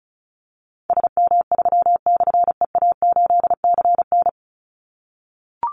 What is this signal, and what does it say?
Morse code, heard as a keyed tone, about 34 wpm, 700 Hz. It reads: SM3/EA8CN
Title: Call Signs - 35wpm